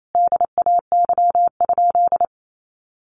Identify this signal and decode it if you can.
Morse code, heard as a keyed tone, about 28 wpm, 700 Hz. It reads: DAY?
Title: Sentences from Top 1100 Words 28wpm